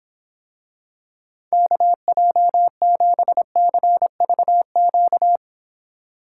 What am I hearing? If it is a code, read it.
Morse code, heard as a keyed tone, about 26 wpm, 700 Hz. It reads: KJ7CVQ